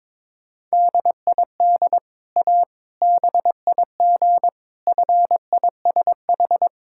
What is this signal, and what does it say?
Morse code, heard as a keyed tone, about 22 wpm, 700 Hz. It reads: DID A BIG FISH
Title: Sentences from Top 1000 Words - Review Part 6 - 22wpm